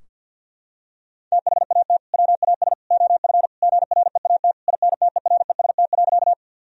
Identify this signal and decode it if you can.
Morse code, heard as a keyed tone, about 50 wpm, 700 Hz. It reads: THAT WAS OF GREAT INTEREST.
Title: Sentences from Top 1000 Words - Review Part 2 - 50wpm